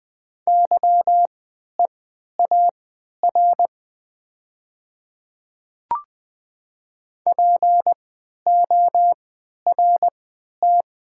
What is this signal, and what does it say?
Morse code, heard as a keyed tone, about 20 wpm, spaced as for 11 wpm, 700 Hz. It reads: YEAR PORT
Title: 3x Character Spacing - Top 200 Words - 20wpm